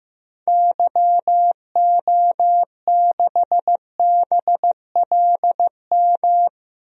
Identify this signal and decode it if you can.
Morse code, heard as a keyed tone, about 15 wpm, 700 Hz. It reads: YO6BLM
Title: DXCC-100 Call Signs - 6 Characters - 15wpm